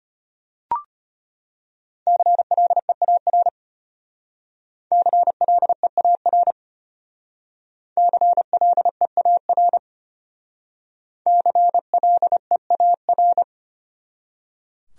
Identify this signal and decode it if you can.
Morse code, heard as a keyed tone, about 33 wpm, 700 Hz. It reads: CLEAR CLEAR CLEAR CLEAR